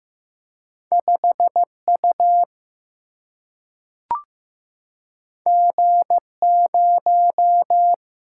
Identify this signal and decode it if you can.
Morse code, heard as a keyed tone, about 15 wpm, 700 Hz. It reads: 5 U G 0
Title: Single-Letter Single-Number 15wpm